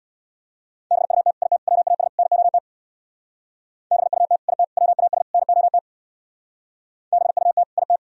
Morse code, heard as a keyed tone, about 62 wpm, 700 Hz. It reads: BUT IT WAS NOT BUT IT WAS NOT BUT IT
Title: Speed-Racing - Sets of 4 Words 50wpm